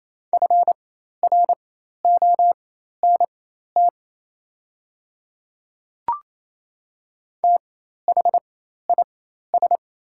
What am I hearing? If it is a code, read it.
Morse code, heard as a keyed tone, about 27 wpm, spaced as for 13 wpm, 700 Hz. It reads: FRONT THIS
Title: 4x Character Spacing - Top 500 Words - 28wpm